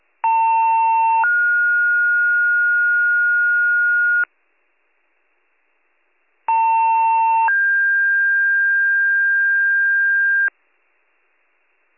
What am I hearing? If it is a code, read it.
Two-tone page, heard as a one-second tone, then a three-second tone, about 900 then 1430 Hz, about 900 then 1600 Hz, with noise. It is heard from 0.2 to 10.5 s.